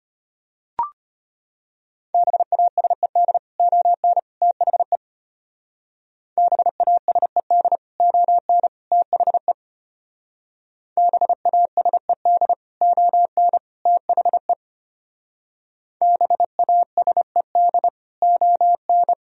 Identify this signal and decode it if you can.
Morse code, heard as a keyed tone, about 29 wpm, 700 Hz. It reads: BASED ON THE BASED ON THE BASED ON THE BASED ON
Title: Speed-Racing - Sets of 3 Words 25wpm